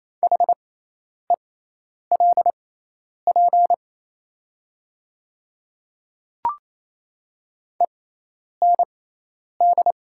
Morse code, heard as a keyed tone, about 27 wpm, spaced as for 10 wpm, 700 Hz. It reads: HELP END
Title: 6x Character Spacing - Top 200 Words - 28wpm